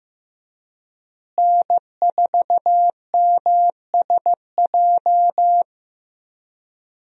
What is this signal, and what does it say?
Morse code, heard as a keyed tone, about 15 wpm, 700 Hz. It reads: N4MSJ
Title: USA Call Signs - 5 Characters - Repeated 2x - 15wpm